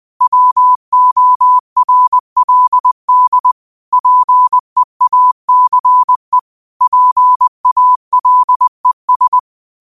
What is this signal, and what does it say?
Morse code, heard as a keyed tone, about 20 wpm, 1000 Hz. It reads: WORLD PEACE PALES